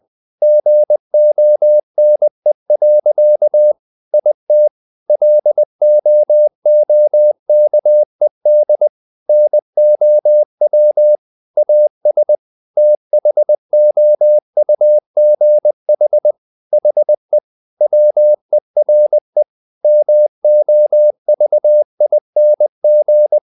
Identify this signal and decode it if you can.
Morse code, heard as a keyed tone, about 20 wpm, 600 Hz. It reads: GONE. IT LOOKED NOW AS THOUGH HE WERE MOVING
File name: the old man and the sea part 2 20wpm